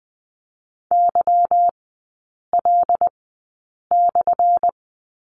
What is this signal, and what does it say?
Morse code, heard as a keyed tone, about 20 wpm, spaced as for 8 wpm, 700 Hz. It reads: YL/
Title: 178 Lesson 29e - Slash - Three Characters 20wpm